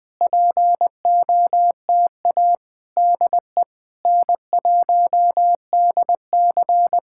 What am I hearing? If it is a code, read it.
Morse code, heard as a keyed tone, about 20 wpm, 700 Hz. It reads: POTA DE N1DC
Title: Speed-Racing Common QSO Phrases - 17wpm